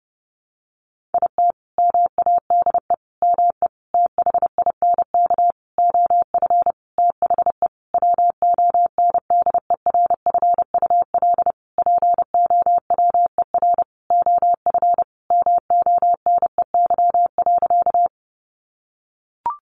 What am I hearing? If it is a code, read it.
Morse code, heard as a keyed tone, about 30 wpm, 700 Hz. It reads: IT MADE ME THINK OF THE WONDERFUL POWER OF MONEY.
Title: Sentences from Top 1100 Words 30wpm